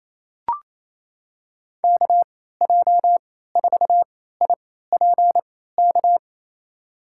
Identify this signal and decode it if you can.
Morse code, heard as a keyed tone, about 28 wpm, spaced as for 16 wpm, 700 Hz. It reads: KJ4IPK
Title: USA Call Signs - 6 Characters - 3x Character Spacing - 28wpm